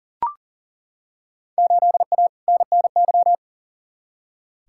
Morse code, heard as a keyed tone, about 40 wpm, 700 Hz. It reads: 8A NNY
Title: Field Day Exchanges Encore - 40wpm